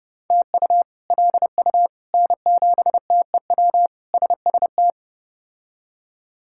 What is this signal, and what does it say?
Morse code, heard as a keyed tone, about 32 wpm, 700 Hz. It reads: TU LU N7TEW SST
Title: K1USN SST QSO Practice - 30wpm